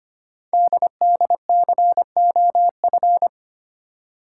Morse code, heard as a keyed tone, about 25 wpm, 700 Hz. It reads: DDCOF
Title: Code Groups - 5 Letters - 25wpm